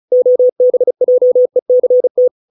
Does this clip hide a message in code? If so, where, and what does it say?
Yes: Morse code from 0.1 to 2.3 s, heard as a keyed tone, about 35 wpm, 500 Hz. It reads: OBJECT